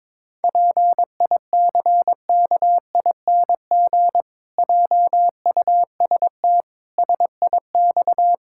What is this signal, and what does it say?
Morse code, heard as a keyed tone, about 22 wpm, 700 Hz. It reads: PICKING JUST SIX
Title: Pangram Challenge v2 - 22wpm